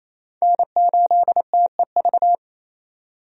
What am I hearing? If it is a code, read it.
Morse code, heard as a keyed tone, about 28 wpm, 700 Hz. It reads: N8TEV
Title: USA Call Signs - 5 Characters - 28wpm